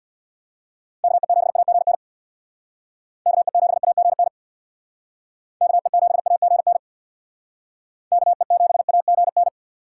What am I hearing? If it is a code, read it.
Morse code, heard as a keyed tone, about 61 wpm, 700 Hz. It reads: KE7AGN KE7AGN KE7AGN KE7AGN
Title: Speed Racing - USA 2x3 Call Signs 50wpm